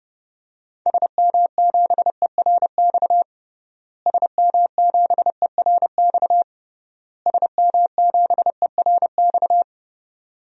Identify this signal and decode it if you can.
Morse code, heard as a keyed tone, about 30 wpm, 700 Hz. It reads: SM7ERX SM7ERX SM7ERX
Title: DXCC-100 Call Signs - 6 Characters - Repeated 3x - 30wpm